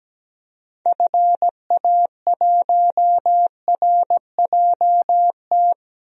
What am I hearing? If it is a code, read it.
Morse code, heard as a keyed tone, about 17 wpm, 700 Hz. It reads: FA1RJT
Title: DXCC-100 Call Signs - 6 Characters - 17wpm